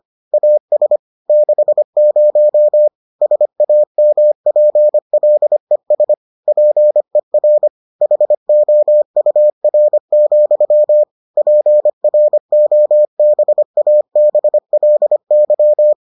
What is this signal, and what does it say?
Morse code, heard as a keyed tone, about 25 wpm, 600 Hz. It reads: AS 60 SAMPLES PER HOUR, PROBABLY